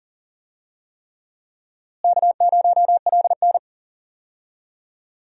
Morse code, heard as a keyed tone, about 40 wpm, 700 Hz. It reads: K0LN
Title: USA Call Signs - 4 Characters - Repeated 3x - 40wpm